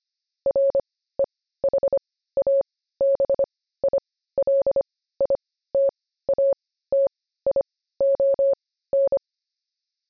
Morse code, heard as a keyed tone, about 25 wpm, spaced as for 15 wpm, 550 Hz. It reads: REHABILITATION
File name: jb recaptured / morse